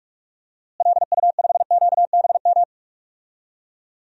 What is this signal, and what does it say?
Morse code, heard as a keyed tone, about 42 wpm, 700 Hz. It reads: LU5QBM
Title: DXCC-100 Call Signs - 6 Characters - Repeated 2x - 45wpm